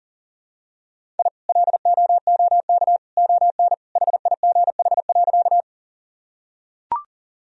Morse code, heard as a keyed tone, about 40 wpm, 700 Hz. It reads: I LOOK ON HIGH.
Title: Sentences from Top 1000 Words - Review Part 1 - 40wpm